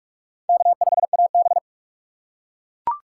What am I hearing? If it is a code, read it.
Morse code, heard as a keyed tone, about 40 wpm, 700 Hz. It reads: K5AB